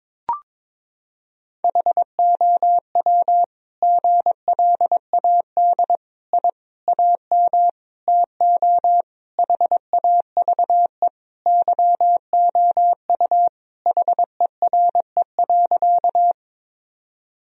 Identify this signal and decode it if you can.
Morse code, heard as a keyed tone, about 22 wpm, 700 Hz. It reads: HOW GLAD I AM TO HAVE YOU HERE.
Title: Sentences from Top 1000 Words - Review Part 2 - 22wpm